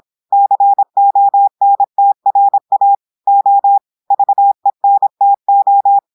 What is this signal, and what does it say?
Morse code, heard as a keyed tone, about 26 wpm, 800 Hz. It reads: CONTRA O VENTO
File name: audios / portugais / portugais texte en clair 1 vitesse 26 mots minutes